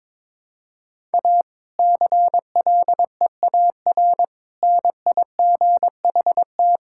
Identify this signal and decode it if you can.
Morse code, heard as a keyed tone, about 22 wpm, 700 Hz. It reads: A CLEAR NIGHT
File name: Difficult Sentences with Hints - 22wpm